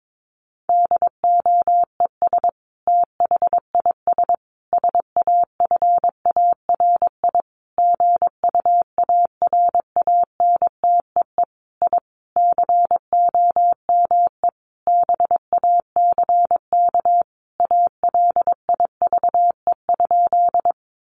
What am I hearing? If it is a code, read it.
Morse code, heard as a keyed tone, about 22 wpm, 700 Hz. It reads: DOES THIS SAFARI GUARANTEE I COME BACK ALIVE?
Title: A Sound Of Thunder 22wpm